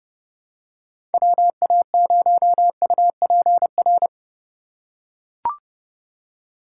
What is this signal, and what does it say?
Morse code, heard as a keyed tone, about 29 wpm, 700 Hz. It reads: WA0UPR